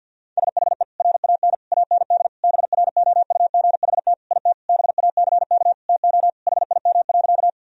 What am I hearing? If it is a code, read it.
Morse code, heard as a keyed tone, about 50 wpm, 700 Hz. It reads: SHE RAN AND BROUGHT IT BACK TO HIM.
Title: Sentences from Top 1000 Words - Review Part 3 - 50wpm